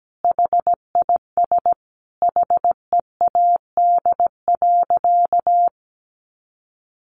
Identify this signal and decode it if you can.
Morse code, heard as a keyed tone, about 17 wpm, 700 Hz. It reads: HIS HEAD.
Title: A Sound Of Thunder 17wpm